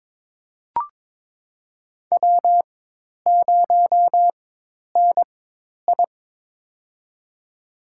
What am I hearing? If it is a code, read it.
Morse code, heard as a keyed tone, about 22 wpm, spaced as for 10 wpm, 700 Hz. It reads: W0NI
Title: USA Call Signs - 4 Characters - 4x Character Spacing - 22wpm